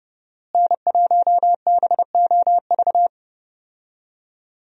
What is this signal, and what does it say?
Morse code, heard as a keyed tone, about 30 wpm, 700 Hz. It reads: N1BOV
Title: USA Call Signs - 5 Characters - Repeated 2x - 30wpm